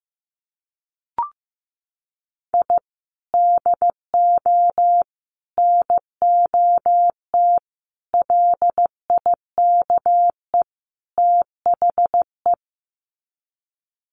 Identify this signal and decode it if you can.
Morse code, heard as a keyed tone, about 15 wpm, 700 Hz. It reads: I DO NOT LIKE THE
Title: Sets of 5 Words Encore 15wpm